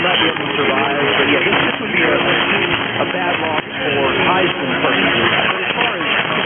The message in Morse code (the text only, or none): none